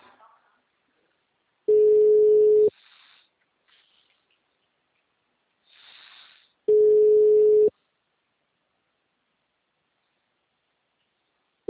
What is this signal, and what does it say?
Ringback tone (Europe, 425 Hz), with speech.